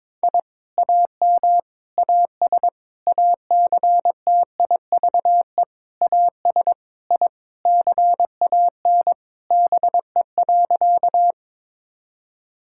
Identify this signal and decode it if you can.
Morse code, heard as a keyed tone, about 21 wpm, 700 Hz. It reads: I AM AS ACTIVE AS I CAN BE.
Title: Sentences from Top 1000 Words - Review Part 6 - 22wpm